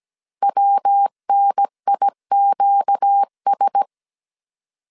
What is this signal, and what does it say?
Morse code, heard as a keyed tone, about 17 wpm, 800 Hz. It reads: WNIQS
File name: 16wpm